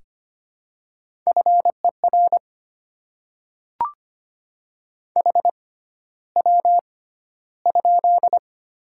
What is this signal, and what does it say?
Morse code, heard as a keyed tone, about 25 wpm, spaced as for 13 wpm, 690 Hz. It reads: FER HW?
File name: Top 20 QSO Elements - 6x Character Spacing - 25wpm